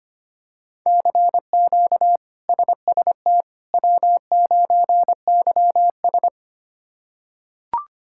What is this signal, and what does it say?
Morse code, heard as a keyed tone, about 25 wpm, 700 Hz. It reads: CQ SST W9YS
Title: K1USN SST QSO Practice - 25wpm